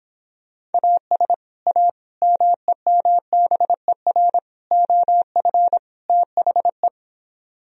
Morse code, heard as a keyed tone, about 26 wpm, 700 Hz. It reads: AS A MEMBER OF THE